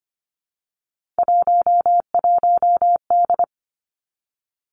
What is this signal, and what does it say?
Morse code, heard as a keyed tone, about 25 wpm, 700 Hz. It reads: 11D
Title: Three Challenging Letter-Number Combinations 25wpm